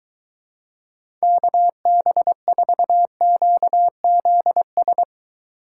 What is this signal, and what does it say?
Morse code, heard as a keyed tone, about 24 wpm, 700 Hz. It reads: KB4QZS